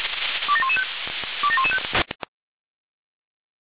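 Five-tone selective call (ZVEI), heard as quick runs of five tones, tones of about 70 ms. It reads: 27115 27115
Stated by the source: ZVEI five-tone selective call received on SDR, address 27036.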